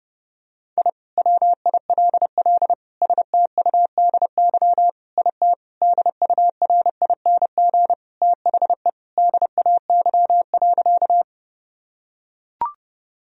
Morse code, heard as a keyed tone, about 30 wpm, 700 Hz. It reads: I WILL STUDY IT DURING THE DAY.
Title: Sentences from Top 1000 Words - Review Part 2 - 30wpm